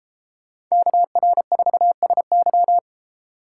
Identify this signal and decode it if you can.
Morse code, heard as a keyed tone, about 32 wpm, 690 Hz. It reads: KR4SY